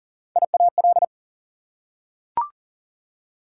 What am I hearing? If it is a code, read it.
Morse code, heard as a keyed tone, about 40 wpm, 700 Hz. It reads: IAL
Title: Prefixes and Suffixes 40wpm